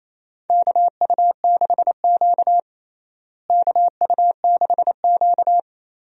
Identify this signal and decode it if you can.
Morse code, heard as a keyed tone, about 28 wpm, 700 Hz. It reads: KU6Q KU6Q
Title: USA Call Signs - 4 Characters - Repeated 3x - 28wpm